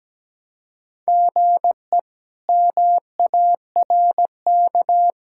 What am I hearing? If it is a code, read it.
Morse code, heard as a keyed tone, about 17 wpm, 700 Hz. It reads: GE MARK
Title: K1USN SST QSO Practice - 17wpm